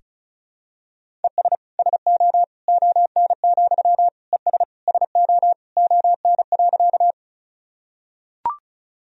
Morse code, heard as a keyed tone, about 34 wpm, 700 Hz. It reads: ES SO ON, ES SO ON.
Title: Sentences from Top 100 Words and Abbreviations 35wpm